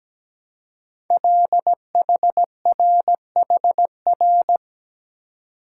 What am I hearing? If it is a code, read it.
Morse code, heard as a keyed tone, about 17 wpm, 700 Hz. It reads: LHRHR
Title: Code Groups - 5 Letter:Numbers - 17wpm